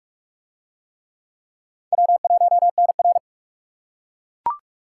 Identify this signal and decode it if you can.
Morse code, heard as a keyed tone, about 37 wpm, 700 Hz. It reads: W1NR
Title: Speed Racing - USA Call Signs 30wpm